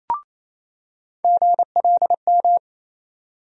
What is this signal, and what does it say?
Morse code, heard as a keyed tone, about 27 wpm, 700 Hz. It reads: GLM
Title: USA Call Sign Postfixes - 3 Letters 28wpm